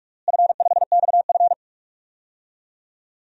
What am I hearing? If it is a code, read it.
Morse code, heard as a keyed tone, about 41 wpm, 700 Hz. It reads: F5XF
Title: DXCC-100 Call Signs - 4 Characters - Repeated 3x - 45wpm